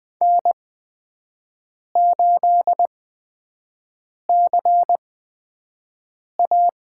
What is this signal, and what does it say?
Morse code, heard as a keyed tone, about 20 wpm, spaced as for 5 wpm, 700 Hz. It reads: N8CA
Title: USA Call Signs - 5 Characters - 8x Character Spacing - 20wpm